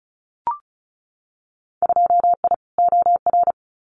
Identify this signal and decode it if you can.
Morse code, heard as a keyed tone, about 34 wpm, 700 Hz. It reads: 2I OR
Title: Winter Field Day Exchanges Encore 35wpm